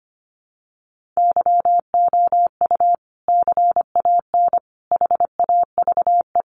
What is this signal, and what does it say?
Morse code, heard as a keyed tone, about 25 wpm, 700 Hz. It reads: YOU CAN HAVE
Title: Sentences from Top 100 Words 25wpm